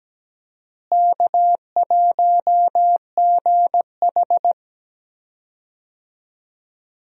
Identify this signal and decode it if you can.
Morse code, heard as a keyed tone, about 17 wpm, 700 Hz. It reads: K1GH
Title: USA Call Signs - 4 Characters - Repeated 3x - 17wpm